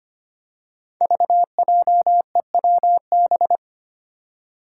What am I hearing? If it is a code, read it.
Morse code, heard as a keyed tone, about 25 wpm, 700 Hz. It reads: VJEWB
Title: Five Challenging Letter-Number Combinations 25wpm